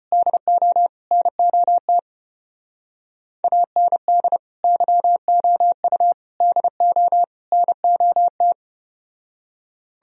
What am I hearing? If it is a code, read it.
Morse code, heard as a keyed tone, about 33 wpm, 700 Hz. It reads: DO NOT AND YOU DO NOT